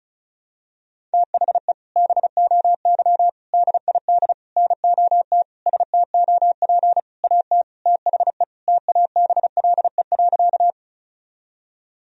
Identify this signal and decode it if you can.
Morse code, heard as a keyed tone, about 34 wpm, 700 Hz. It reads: THE BOY DID NOT STOP AT THE TABLE.
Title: Sentences from Top 1000 Words - Review Part 2 - 35wpm